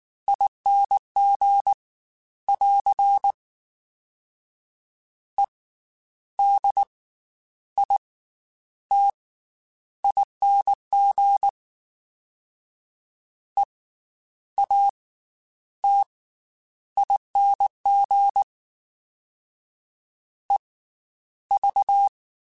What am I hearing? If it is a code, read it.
Morse code, heard as a keyed tone, about 19 wpm, 800 Hz. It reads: ING <AR> E D I T ING E A T ING E V
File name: Special INGs / Special INGs-mp3s / Slow ing @18